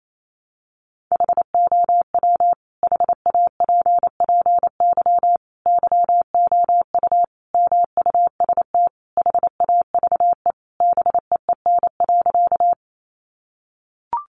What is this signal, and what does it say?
Morse code, heard as a keyed tone, about 28 wpm, 700 Hz. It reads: HOW HAPPY YOU MUST HAVE BEEN.
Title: Sentences from Top 1000 Words 28wpm